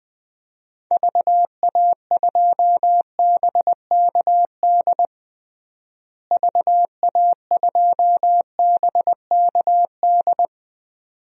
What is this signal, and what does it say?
Morse code, heard as a keyed tone, about 21 wpm, 700 Hz. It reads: VA2BKD VA2BKD
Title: DXCC-100 Call Signs - 6 Characters - Repeated 2x - 20wpm